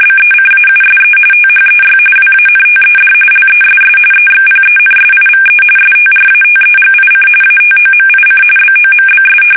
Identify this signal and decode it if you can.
RTTY, about 45.45 baud. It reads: BECAUSE THEY WANTED LESS EXPENSIVE STRATEGIC ATOMIC BOMBS
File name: fsk850 / usnavy5